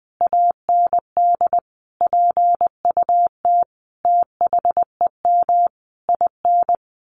Morse code, heard as a keyed tone, about 20 wpm, 700 Hz. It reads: AND PUT THEM IN
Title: Sentences from Top 1300 Words 20wpm